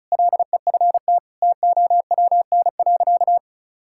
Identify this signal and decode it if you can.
Morse code, heard as a keyed tone, about 34 wpm, 700 Hz. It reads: LEFT TOWN.
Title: Sentences from Top 1000 Words - Review Part 6 - 35wpm